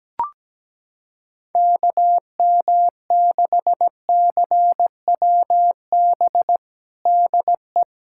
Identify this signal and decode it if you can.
Morse code, heard as a keyed tone, about 17 wpm, 700 Hz. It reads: KM6CWB DE
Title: USA Callsign DE Callsign 17wpm